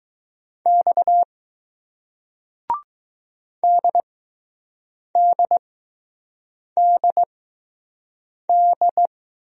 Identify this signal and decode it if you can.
Morse code, heard as a keyed tone, about 21 wpm, spaced as for 6 wpm, 700 Hz. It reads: X DDDD